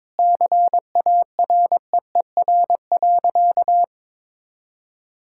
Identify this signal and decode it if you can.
Morse code, heard as a keyed tone, about 21 wpm, 700 Hz. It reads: CAREER.
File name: Sentences with 7-Word Phrases - 22wpm